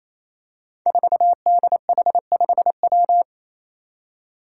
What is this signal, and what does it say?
Morse code, heard as a keyed tone, about 27 wpm, 700 Hz. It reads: 4DH5W
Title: Five Challenging Letter-Number Combinations 28wpm